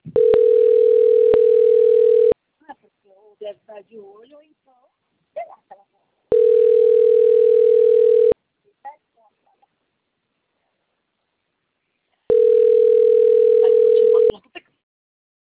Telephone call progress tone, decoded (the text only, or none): ringback tone